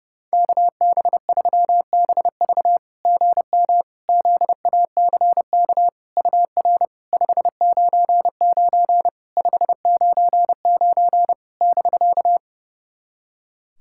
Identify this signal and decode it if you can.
Morse code, heard as a keyed tone, about 30 wpm, 700 Hz. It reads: KB3BV GM ZACK UR 599 599 <BK>
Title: SOTA QSO Practice 30wpm